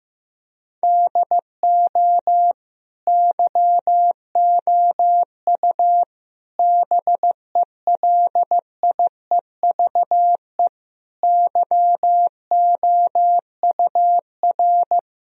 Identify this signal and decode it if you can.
Morse code, heard as a keyed tone, about 16 wpm, 700 Hz. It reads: DO YOU BELIEVE YOUR
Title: Sentences from Top 1000 Words - Review Part 3 - 15wpm